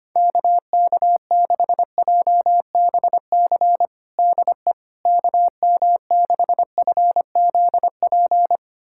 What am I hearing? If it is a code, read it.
Morse code, heard as a keyed tone, about 25 wpm, 700 Hz. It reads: KK6JBC DE KM6FZP
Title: USA Callsign DE Callsign - 6 Characters - 25wpm